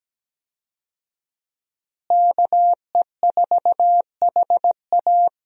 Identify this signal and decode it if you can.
Morse code, heard as a keyed tone, about 17 wpm, 700 Hz. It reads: KE4HA
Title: USA Call Signs - 5 Characters - Repeated 3x - 17wpm